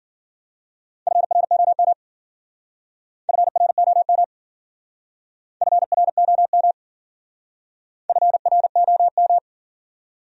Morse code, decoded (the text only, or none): FROM FROM FROM FROM